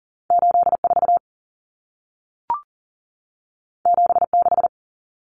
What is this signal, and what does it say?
Morse code, heard as a keyed tone, about 40 wpm, 700 Hz. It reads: 84 76
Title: Numbers - 2-Digit 40wpm